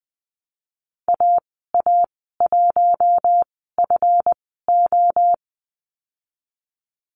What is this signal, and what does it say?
Morse code, heard as a keyed tone, about 20 wpm, spaced as for 15 wpm, 700 Hz. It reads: AA1FO
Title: 154 Lesson 25g - Letter p - Call Signs 20wpm